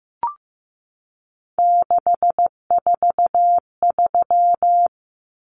Morse code, heard as a keyed tone, about 15 wpm, 700 Hz. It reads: 643
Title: Common Cut Numbers - 3-Digit 15wpm